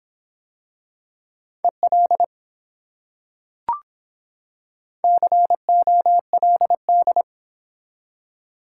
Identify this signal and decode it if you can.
Morse code, heard as a keyed tone, about 24 wpm, 700 Hz. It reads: EL COLD